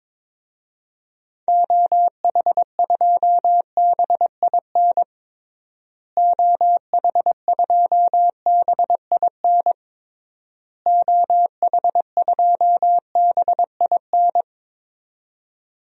Morse code, heard as a keyed tone, about 22 wpm, 700 Hz. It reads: OH2BIN OH2BIN OH2BIN
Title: DXCC-100 Call Signs - 6 Characters - Repeated 3x - 22wpm